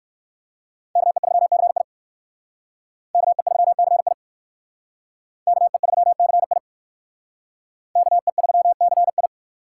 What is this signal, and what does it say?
Morse code, heard as a keyed tone, about 56 wpm, 700 Hz. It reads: KE3CI KE3CI KE3CI KE3CI